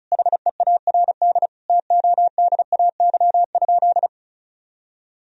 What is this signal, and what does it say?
Morse code, heard as a keyed tone, about 34 wpm, 700 Hz. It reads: HEARD TODAY?
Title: Sentences from Top 1000 Words - Review Part 4 - 35wpm